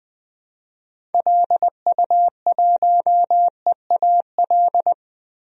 Morse code, heard as a keyed tone, about 20 wpm, 700 Hz. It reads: LU1EAL
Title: DXCC-100 Call Signs - 6 Characters - 20wpm